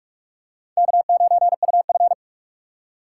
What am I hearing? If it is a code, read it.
Morse code, heard as a keyed tone, about 46 wpm, 700 Hz. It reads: K9UF
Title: USA Call Signs - 4 Characters - 4x Character Spacing - 45wpm